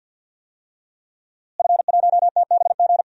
Morse code, heard as a keyed tone, about 50 wpm, 700 Hz. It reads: F1TBG
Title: DXCC-100 Call Signs - 5 Characters - Repeated 3x - 50wpm